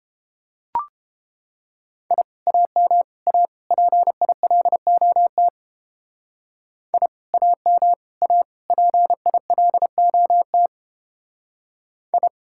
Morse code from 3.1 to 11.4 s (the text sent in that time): A PILOT I AM A PILOT